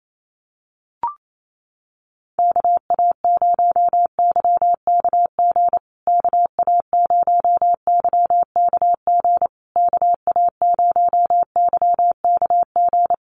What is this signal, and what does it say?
Morse code, heard as a keyed tone, about 28 wpm, 700 Hz. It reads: KA0YKG KA0YKG KA0YKG
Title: USA Call Signs Repeated 3x 28wpm